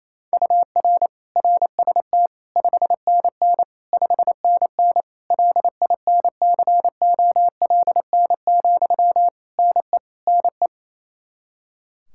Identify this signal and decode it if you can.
Morse code, heard as a keyed tone, about 28 wpm, 700 Hz. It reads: UR RST 5NN 5NN LINCOLN, NE NE
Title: Practice QSO - Exchange 1 - Beginning - 28wpm